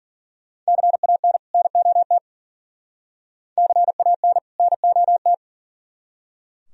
Morse code, heard as a keyed tone, about 50 wpm, 700 Hz. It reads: CAN NOT CAN NOT